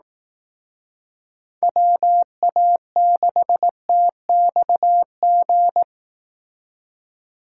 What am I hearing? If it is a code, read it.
Morse code, heard as a keyed tone, about 18 wpm, 700 Hz. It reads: WA6TXG
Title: Speed Racing - USA 2x3 Call Signs 15wpm